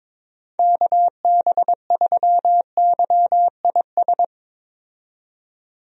Morse code, heard as a keyed tone, about 22 wpm, 700 Hz. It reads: KB3YIS
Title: USA Call Signs - 6 Characters - 22wpm